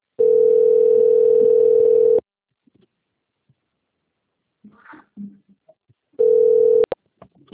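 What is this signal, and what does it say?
Ringback tone (North America), 440 + 480 Hz.
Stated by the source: Outgoing phone call recorded on the Itatiba 3 phone system.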